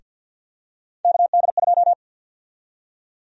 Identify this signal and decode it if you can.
Morse code, heard as a keyed tone, about 52 wpm, 700 Hz. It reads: KD2
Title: USA Call Sign Prefixes - Repeated 2x - 50wpm